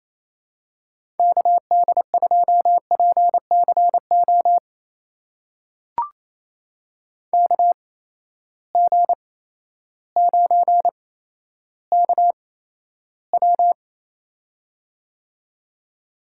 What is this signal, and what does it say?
Morse code, heard as a keyed tone, about 28 wpm, spaced as for 13 wpm, 700 Hz. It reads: KD2PCO KG9KW
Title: USA Call Signs - 8x Character Spacing - 28wpm